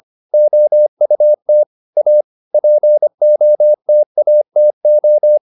Morse code, heard as a keyed tone, about 25 wpm, 600 Hz. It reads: OUT A POTATO